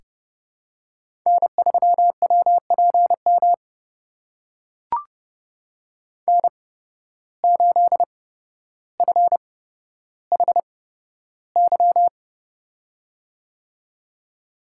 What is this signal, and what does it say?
Morse code, heard as a keyed tone, about 31 wpm, spaced as for 13 wpm, 700 Hz. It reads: N3WPM N8FHY
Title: USA Call Signs - 5 Characters - 8x Character Spacing - 30wpm